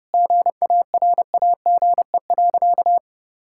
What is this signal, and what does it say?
Morse code, heard as a keyed tone, about 30 wpm, 700 Hz. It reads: GARAGE.
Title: Sentences from Top 1600 Words 30wpm